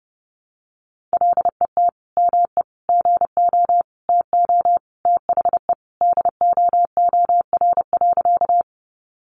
Morse code, heard as a keyed tone, about 30 wpm, 700 Hz. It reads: LET ME GO TO THE DOOR.
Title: Sentences from Top 300 Words 30wpm